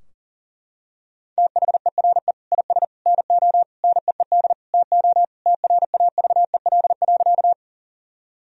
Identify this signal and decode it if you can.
Morse code, heard as a keyed tone, about 40 wpm, 700 Hz. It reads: THERE IS NO NEED TO TRAVEL.
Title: Sentences from Top 1000 Words - Review Part 2 - 40wpm